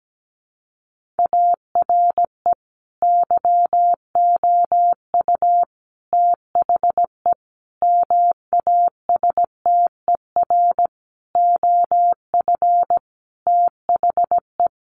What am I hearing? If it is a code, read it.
Morse code, heard as a keyed tone, about 17 wpm, 700 Hz. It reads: ARE YOU THE MASTER OF THE
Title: Sentences from Top 1300 Words 17wpm